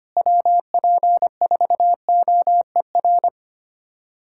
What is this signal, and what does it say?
Morse code, heard as a keyed tone, about 25 wpm, 700 Hz. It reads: WP4OER
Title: USA Callsign DE Callsign - 6 Characters - 25wpm